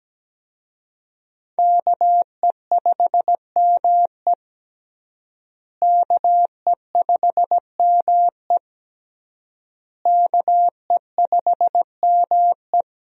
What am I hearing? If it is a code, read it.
Morse code, heard as a keyed tone, about 17 wpm, 700 Hz. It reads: KE5ME KE5ME KE5ME
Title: USA Call Signs - 5 Characters - Repeated 3x - 17wpm